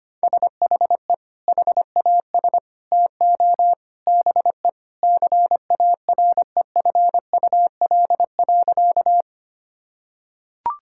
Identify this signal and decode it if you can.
Morse code, heard as a keyed tone, about 25 wpm, 700 Hz. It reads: SHE HAS TO BE CAREFUL.
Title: Sentences from Top 1600 Words 25wpm